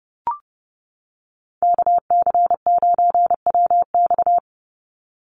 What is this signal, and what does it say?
Morse code, heard as a keyed tone, about 30 wpm, 700 Hz. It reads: KC9WX
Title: USA Call Signs 30wpm